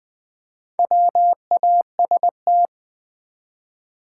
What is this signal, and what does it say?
Morse code, heard as a keyed tone, about 20 wpm, 700 Hz. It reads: WAST
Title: Build A Word - 1 Letter At A Time - 5-Letter Words - 20wpm